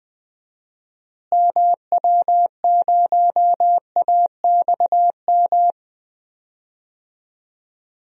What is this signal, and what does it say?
Morse code, heard as a keyed tone, about 20 wpm, 700 Hz. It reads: MW0AXM
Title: DXCC-100 Call Signs - 6 Characters - Repeated 2x - 20wpm